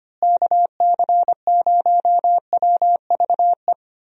Morse code, heard as a keyed tone, about 25 wpm, 700 Hz. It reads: KC0WVE
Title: USA Call Signs - 6 Characters - Repeated 2x - 25wpm